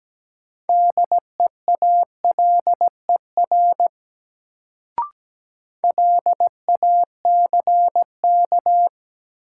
Morse code, heard as a keyed tone, about 17 wpm, 700 Hz. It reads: DEALER LACK
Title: Sentences from Top 1100 Words - New Word Drill - 17wpm